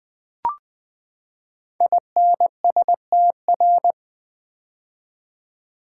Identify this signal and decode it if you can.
Morse code, heard as a keyed tone, about 20 wpm, 700 Hz. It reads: INSTR
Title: Build A Word - 5 Letters At A Time - 10-Letter Words - 20wpm